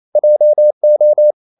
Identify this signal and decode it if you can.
Morse code, heard as a keyed tone, about 28 wpm, 600 Hz. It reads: JO
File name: Ferie QRQ 28wpm